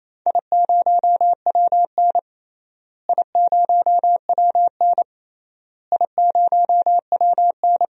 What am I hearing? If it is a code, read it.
Morse code, heard as a keyed tone, about 28 wpm, 700 Hz. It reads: I0WN I0WN I0WN
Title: DXCC-100 Call Signs - 4 Characters - Repeated 3x - 28wpm